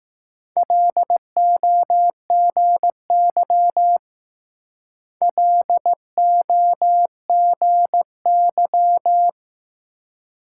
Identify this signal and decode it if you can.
Morse code, heard as a keyed tone, about 19 wpm, 700 Hz. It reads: LOGY LOGY